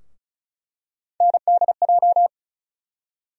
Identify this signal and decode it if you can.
Morse code, heard as a keyed tone, about 36 wpm, 700 Hz. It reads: NDJ
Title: USA Call Sign Postfixes - 3 Letters 35wpm